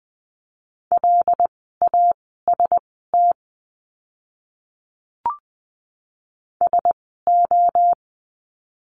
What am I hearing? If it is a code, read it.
Morse code, heard as a keyed tone, about 20 wpm, spaced as for 15 wpm, 700 Hz. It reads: LAST SO
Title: 244 Lesson 39f - Letter z - Words 20wpm